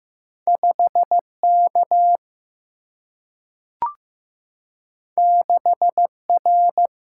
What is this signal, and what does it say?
Morse code, heard as a keyed tone, about 15 wpm, 700 Hz. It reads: 5K 6R
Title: Single-Letter Single-Number 15wpm